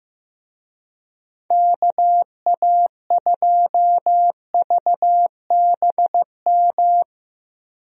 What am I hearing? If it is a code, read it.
Morse code, heard as a keyed tone, about 15 wpm, 700 Hz. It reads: KA2VBM
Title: USA Call Signs - 6 Characters - Repeated 2x - 15wpm